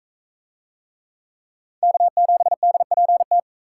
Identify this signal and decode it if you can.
Morse code, heard as a keyed tone, about 41 wpm, 700 Hz. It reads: K7DPT